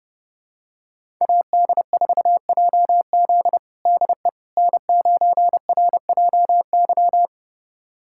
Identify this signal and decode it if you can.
Morse code, heard as a keyed tone, about 30 wpm, 700 Hz. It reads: AD4JZ DE N9RJY
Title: USA Callsign DE Callsign - 5 Characters - 30wpm